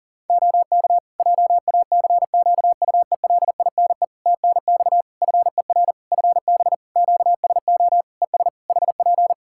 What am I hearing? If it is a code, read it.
Morse code, heard as a keyed tone, about 40 wpm, 700 Hz. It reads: OK JACQUELINE TNX FER FB QSO ES HP
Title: K1USN SST QSO Practice 40wpm